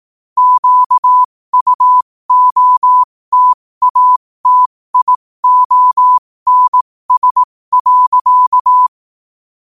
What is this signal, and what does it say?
Morse code, heard as a keyed tone, about 18 wpm, spaced as for 15 wpm, 1000 Hz. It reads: QUOTATIONS.
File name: qotdfiles / 15wpm